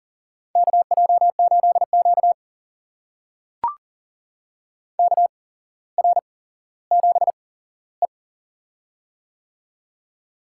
Morse code, heard as a keyed tone, about 41 wpm, spaced as for 17 wpm, 700 Hz. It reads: KJ8Q KR7E
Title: USA Call Signs - 4 Characters - 8x Character Spacing - 40wpm